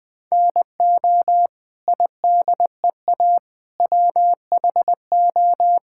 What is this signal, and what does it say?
Morse code, heard as a keyed tone, about 20 wpm, 700 Hz. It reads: NO IDEA WHO
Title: Sets of 9 Words - 20wpm